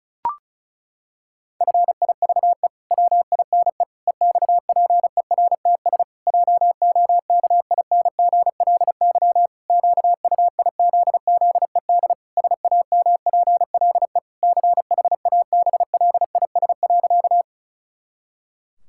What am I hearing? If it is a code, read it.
Morse code, heard as a keyed tone, about 34 wpm, 700 Hz. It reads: FIVE WINE EXPERTS JOKINGLY QUIZZED SAMPLE CHABLIS.
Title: Pangram Challenge v2 - 35wpm